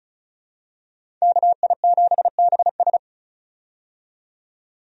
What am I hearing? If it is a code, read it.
Morse code, heard as a keyed tone, about 36 wpm, 700 Hz. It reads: KI7BS
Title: USA Call Signs - 5 Characters - 35wpm